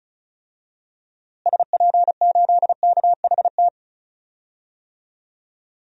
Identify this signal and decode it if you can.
Morse code, heard as a keyed tone, about 36 wpm, 700 Hz. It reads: SP8KHT
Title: DXCC-100 Call Signs - 6 Characters - Repeated 2x - 35wpm